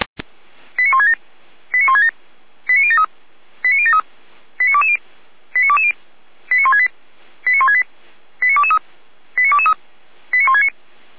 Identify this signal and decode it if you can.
Five-tone selective call (ZVEI), heard as quick runs of five tones, tones of about 80 ms. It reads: 89167 89167 89073 89073 89220 89220 89168 89168 89223 89223 89169